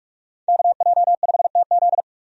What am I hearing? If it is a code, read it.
Morse code, heard as a keyed tone, about 46 wpm, 700 Hz. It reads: KJ5TZ
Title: USA Call Signs - 5 Characters - 45wpm